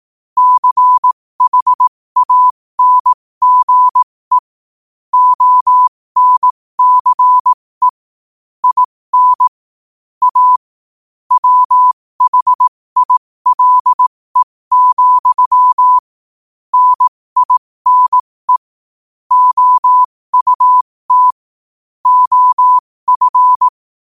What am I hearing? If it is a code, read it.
Morse code, heard as a keyed tone, about 18 wpm, spaced as for 15 wpm, 1000 Hz. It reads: CHANGE ONCE IN A WHILE, NINE OUT OF
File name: qotdfiles / 15wpm